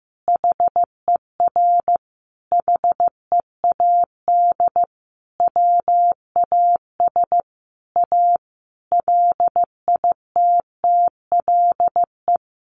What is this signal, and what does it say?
Morse code, heard as a keyed tone, about 15 wpm, 700 Hz. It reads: HER HEAD WAS A LITTLE
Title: Sentences from Top 1200 Words 15wpm